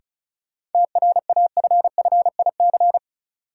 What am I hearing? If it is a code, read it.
Morse code, heard as a keyed tone, about 34 wpm, 700 Hz. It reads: TRAFFIC
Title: Sentences from Top 1300 Words - New Word Drill Alt - 35wpm